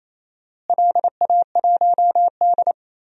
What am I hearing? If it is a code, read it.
Morse code, heard as a keyed tone, about 28 wpm, 700 Hz. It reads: LA1D
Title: DXCC-100 Call Signs - 4 Characters - Repeated 3x - 28wpm